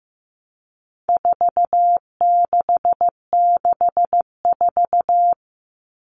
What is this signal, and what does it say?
Morse code, heard as a keyed tone, about 15 wpm, 700 Hz. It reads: 4664
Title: All Cut Numbers - 4-Digit 15wpm